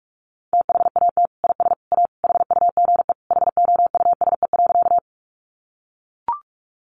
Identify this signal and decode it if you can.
Morse code, heard as a keyed tone, about 44 wpm, 700 Hz. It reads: THAT IS A HUGE HOUSE.
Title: Sentences from Top 1100 Words 45wpm